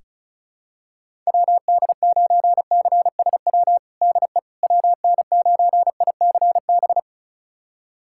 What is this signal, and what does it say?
Morse code, heard as a keyed tone, about 36 wpm, 700 Hz. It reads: WD9CSW DE WN9ICB
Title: USA Callsign DE Callsign 35wpm